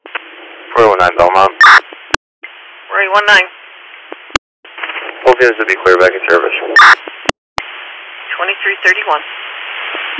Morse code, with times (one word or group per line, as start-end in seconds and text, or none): none